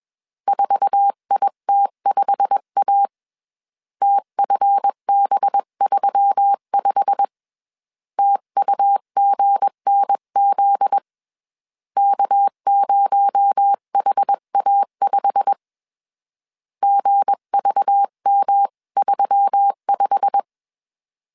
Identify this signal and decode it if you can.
Morse code, heard as a keyed tone, about 21 wpm, 800 Hz. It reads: 4IT5A TFB35 TUGNZ K0HA5 GVM35
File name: CW / DET / 1 9 1500C S B 20wpm 2313